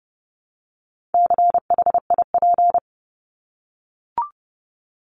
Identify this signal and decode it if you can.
Morse code, heard as a keyed tone, about 30 wpm, 700 Hz. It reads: CHIP
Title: Top 1751-2000 Words 30wpm